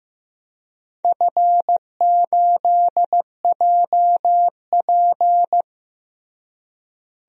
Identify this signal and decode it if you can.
Morse code, heard as a keyed tone, about 15 wpm, 700 Hz. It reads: F8JP
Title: DXCC-100 Call Signs - 4 Characters - Repeated 2x - 15wpm